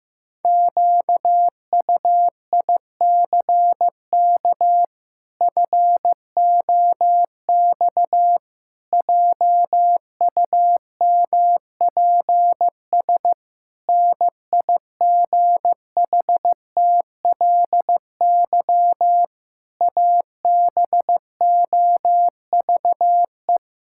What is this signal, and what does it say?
Morse code, heard as a keyed tone, about 16 wpm, 700 Hz. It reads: QUICK FOX JUMPS NIGHTLY ABOVE
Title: Pangram Challenge v2 - 15wpm